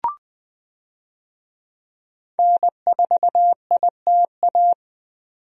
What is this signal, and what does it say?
Morse code, heard as a keyed tone, about 20 wpm, 700 Hz. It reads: N4ITA
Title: USA Call Signs - 5 Characters - 20wpm